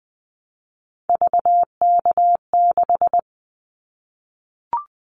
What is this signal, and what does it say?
Morse code, heard as a keyed tone, about 20 wpm, 700 Hz. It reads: VK6
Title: Callsign Prefixes for Top 50 DXCC - 20wpm